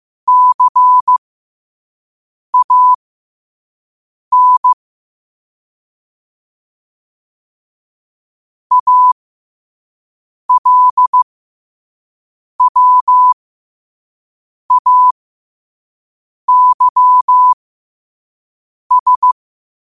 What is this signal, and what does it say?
Morse code, heard as a keyed tone, about 15 wpm, spaced as for 5 wpm, 1000 Hz. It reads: CAN ALWAYS
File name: qotdfiles / 5wpm